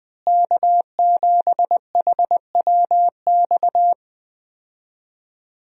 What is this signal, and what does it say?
Morse code, heard as a keyed tone, about 21 wpm, 700 Hz. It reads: K7HWX
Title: USA Call Signs - 5 Characters - Repeated 3x - 20wpm